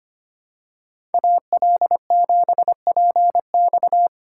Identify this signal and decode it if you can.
Morse code, heard as a keyed tone, about 25 wpm, 700 Hz. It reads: AL7PX
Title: USA Call Signs - 5 Characters - 2x Character Spacing - 25wpm